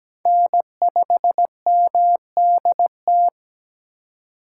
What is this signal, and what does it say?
Morse code, heard as a keyed tone, about 17 wpm, 700 Hz. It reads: N5MDT
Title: USA Call Signs - 5 Characters - 3x Character Spacing - 17wpm